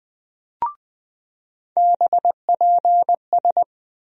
Morse code, heard as a keyed tone, about 20 wpm, 700 Hz. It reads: BPS
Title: USA Call Sign Postfixes - 3 Letters 20wpm